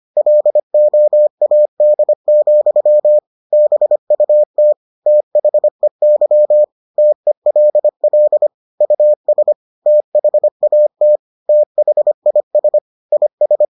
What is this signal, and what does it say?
Morse code, heard as a keyed tone, about 25 wpm, 600 Hz. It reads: LOAD, BUT THEY TELL US THAT THIS IS